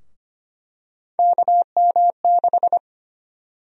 Morse code, heard as a keyed tone, about 25 wpm, 700 Hz. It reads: KM6
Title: USA Call Sign Prefixes - 25wpm